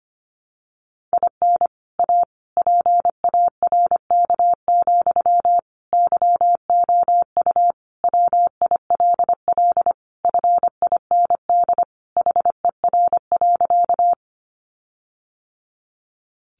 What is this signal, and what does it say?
Morse code, heard as a keyed tone, about 25 wpm, 700 Hz. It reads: IN A PARK, YOU WILL FIND HER.
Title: Sentences from Top 1300 Words 25wpm